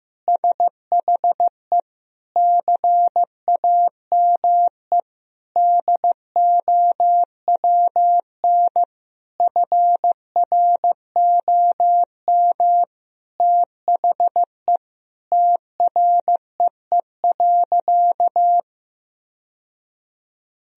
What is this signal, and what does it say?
Morse code, heard as a keyed tone, about 15 wpm, 700 Hz. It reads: SHE CAME DOWN FROM THE TREE.
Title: Sentences from Top 1000 Words - Review Part 1 - 15wpm